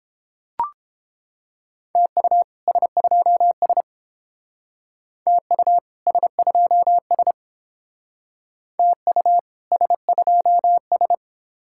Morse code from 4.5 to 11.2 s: TU S2S TU S2S

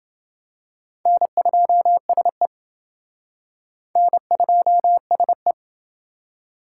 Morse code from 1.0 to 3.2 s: N2SE

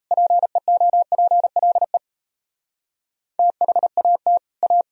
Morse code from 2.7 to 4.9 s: THAT A

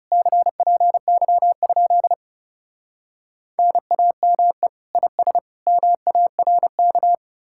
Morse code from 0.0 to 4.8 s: CPY? NAME

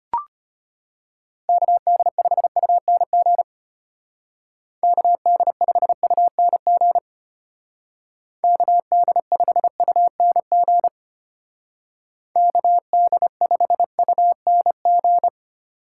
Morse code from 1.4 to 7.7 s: KD5UNG KD5UNG

